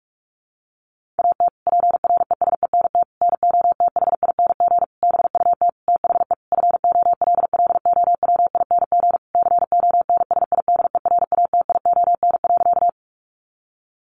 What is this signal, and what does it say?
Morse code, heard as a keyed tone, about 44 wpm, 700 Hz. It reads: AT PRESENT NOTHING BUT THE FOLLOWING CONSIDERATION.
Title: Sentences from Top 1500 Words 45wpm